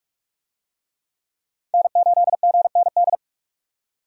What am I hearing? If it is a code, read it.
Morse code, heard as a keyed tone, about 40 wpm, 700 Hz. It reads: N8GND